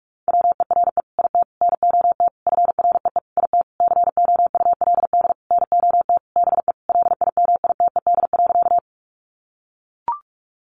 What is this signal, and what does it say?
Morse code, heard as a keyed tone, about 44 wpm, 700 Hz. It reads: WERE IT NOT FREE IT COULD NOT BE LIMITED.
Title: Sentences from Top 900 Words 45wpm